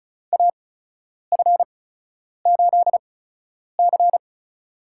Morse code, heard as a keyed tone, about 36 wpm, spaced as for 10 wpm, 700 Hz. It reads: AF8C
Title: USA Call Signs - 4 Characters - 8x Character Spacing - 35wpm